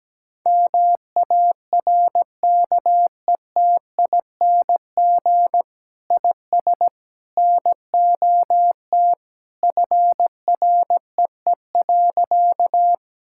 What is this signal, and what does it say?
Morse code, heard as a keyed tone, about 17 wpm, 700 Hz. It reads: MARKETING IS NOT FREE.
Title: Sentences from Top 1000 Words - Review Part 5 - 17wpm